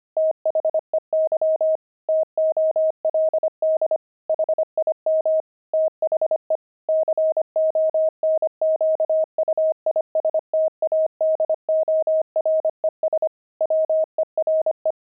Morse code, heard as a keyed tone, about 25 wpm, 630 Hz. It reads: THEY TOLD HIM THE CONQUISTADORES WERE
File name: pangram-fs-25-cs-25-kidding